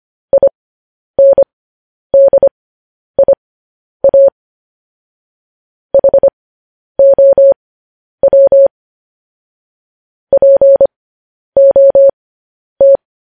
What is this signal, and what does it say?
Morse code, heard as a keyed tone, about 25 wpm, spaced as for 10 wpm, 550 Hz. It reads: INDIA HOW POT
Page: News Headlines in Morse Code at 10 WPM